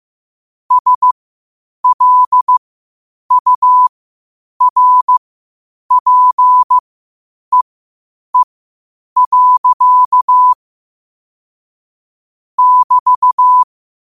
Morse code, heard as a keyed tone, about 15 wpm, spaced as for 8 wpm, 1000 Hz. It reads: SLURPEE. <BT>
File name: 8wpm